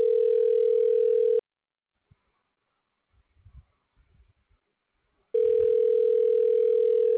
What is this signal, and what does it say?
Ringback tone (North America), 440 + 480 Hz.